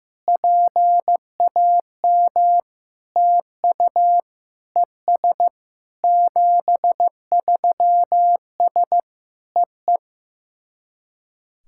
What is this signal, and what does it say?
Morse code, heard as a keyed tone, about 14 wpm, 700 Hz. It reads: PAM TU ES 73S EE
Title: SOTA QSO Practice 15wpm